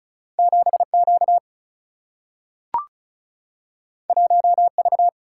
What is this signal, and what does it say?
Morse code, heard as a keyed tone, about 36 wpm, 700 Hz. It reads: 7 Q 1 V
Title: Single-Letter Single-Number 35wpm